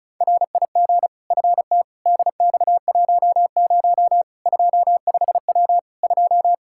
Morse code, heard as a keyed tone, about 36 wpm, 700 Hz. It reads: RIG FT DX10 25W 2
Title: Practice QSO - Exchange 2 - Info - 35wpm